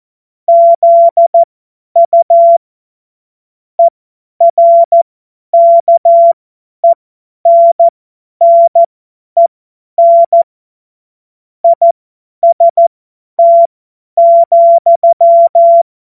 Morse code, heard as a keyed tone, about 14 wpm, spaced as for 10 wpm, 680 Hz. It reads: ZU ERKENNEN IST,
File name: Morsekurs / Witze / Witz 14. Juni 2022 50 F70